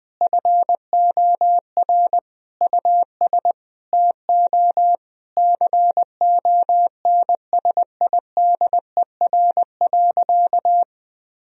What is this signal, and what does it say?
Morse code, heard as a keyed tone, about 20 wpm, 700 Hz. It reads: FOR US TO CONSIDER.
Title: Sentences from Top 1000 Words - Review Part 4 - 20wpm